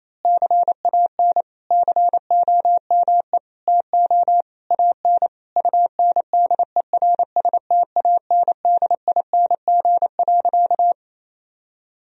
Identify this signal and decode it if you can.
Morse code, heard as a keyed tone, about 28 wpm, 700 Hz. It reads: CAN COME TO AN UNDERSTANDING.
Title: Sentences from Top 1000 Words - Review Part 3 - 28wpm